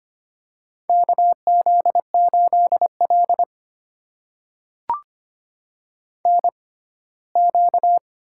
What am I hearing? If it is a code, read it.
Morse code, heard as a keyed tone, about 25 wpm, spaced as for 17 wpm, 700 Hz. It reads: KZ8L NQ
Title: USA Call Signs - 6x Character Spacing - 25wpm